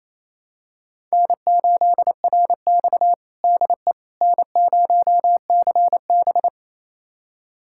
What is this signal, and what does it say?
Morse code, heard as a keyed tone, about 28 wpm, 700 Hz. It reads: N8RX DE N0CB
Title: USA Callsign DE Callsign - 4 Characters - 28wpm